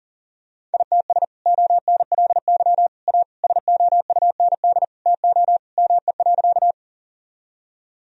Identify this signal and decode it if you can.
Morse code, heard as a keyed tone, about 40 wpm, 700 Hz. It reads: ITS ONLY A SOUND TO ME.
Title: Sentences from Top 1000 Words - Review Part 1 - 40wpm